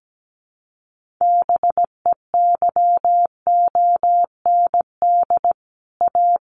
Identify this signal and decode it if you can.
Morse code, heard as a keyed tone, about 17 wpm, 700 Hz. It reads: BEYOND A
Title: Sentences from Top 1500 Words 17wpm